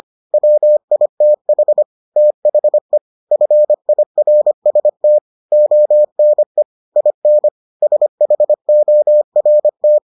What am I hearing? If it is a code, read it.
Morse code, heard as a keyed tone, about 25 wpm, 600 Hz. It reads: WITH THE FIRST ONE IN SHORT